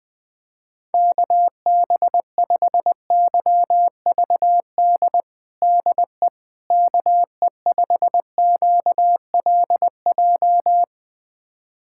Morse code, heard as a keyed tone, about 20 wpm, 700 Hz. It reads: KB5YVD DE KE5QLJ
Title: USA Callsign DE Callsign - 6 Characters - 20wpm